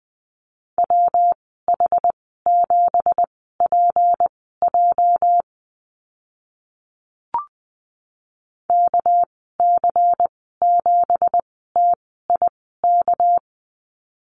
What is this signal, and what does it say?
Morse code, heard as a keyed tone, about 20 wpm, spaced as for 14 wpm, 700 Hz. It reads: WH7PJ KC7TIK
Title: 219 Lesson 35g - Number 0 - Call Signs 20wpm